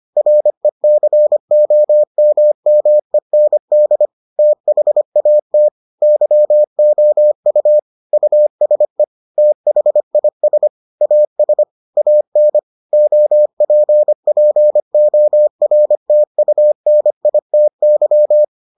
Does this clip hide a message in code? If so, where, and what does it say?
Yes: Morse code from 0.2 to 18.4 s, heard as a keyed tone, about 25 wpm, 600 Hz. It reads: RECOMMEND THAT YOU USE THIS AS AN OPPORTUNITY